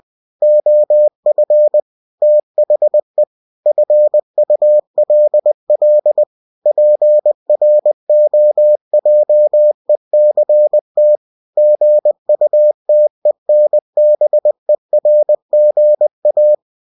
Morse code, heard as a keyed tone, about 20 wpm, 600 Hz. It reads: OF THE FULL PROJECT GUTENBERGA
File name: cw-books / winnie-the-pooh / winnie-20wpm-page-0113